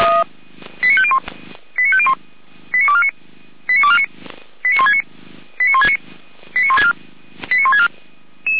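Five-tone selective call (ZVEI), heard as quick runs of five tones, tones of about 70 ms. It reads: 89501 89501 89249 89249 89169 89169 89164 89164
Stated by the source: ZVEI five-tone selective call received on an SDR.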